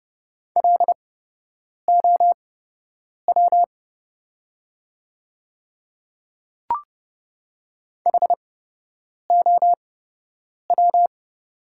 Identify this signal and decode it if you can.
Morse code, heard as a keyed tone, about 30 wpm, spaced as for 8 wpm, 700 Hz. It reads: LOW HOW
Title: 8x Character Spacing - Top 300 Words - 30wpm